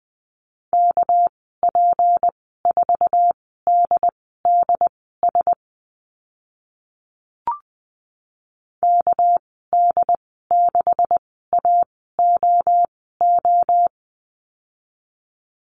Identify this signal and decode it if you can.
Morse code, heard as a keyed tone, about 20 wpm, spaced as for 14 wpm, 700 Hz. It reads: KP4DDS KD6AOO
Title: 245 Lesson 39g - Letter z - Call Signs 20wpm